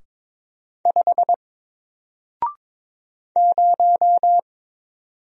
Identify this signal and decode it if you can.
Morse code, heard as a keyed tone, about 22 wpm, 700 Hz. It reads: 5 0
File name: Single-Letter-Number-22wpm